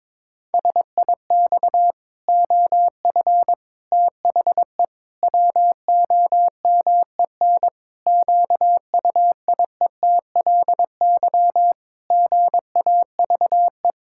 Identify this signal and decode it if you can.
Morse code, heard as a keyed tone, about 22 wpm, 700 Hz. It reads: SIX OF THE WOMEN QUIETLY GAVE
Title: Pangram Challenge v2 - 22wpm